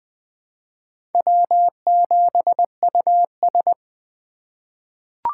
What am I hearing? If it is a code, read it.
Morse code, heard as a keyed tone, about 20 wpm, 700 Hz. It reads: W7US
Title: USA Call Signs - 4 Characters - 2x Character Spacing - 20wpm